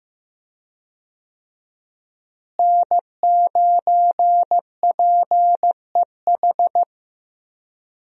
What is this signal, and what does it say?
Morse code, heard as a keyed tone, about 15 wpm, 700 Hz. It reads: N9PEH
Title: USA Call Signs - 5 Characters - Repeated 2x - 15wpm